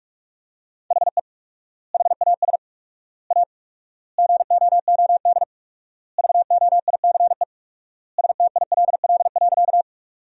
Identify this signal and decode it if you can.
Morse code, heard as a keyed tone, about 42 wpm, 690 Hz. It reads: HE HAS A GOOD VOICE STILL.
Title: 4x Word-Spacing - Sentences from Top 500 Words - 45wpm